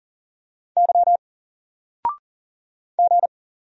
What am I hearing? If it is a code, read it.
Morse code, heard as a keyed tone, about 40 wpm, 700 Hz. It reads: Y G
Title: Single Letter-Number Alt 40wpm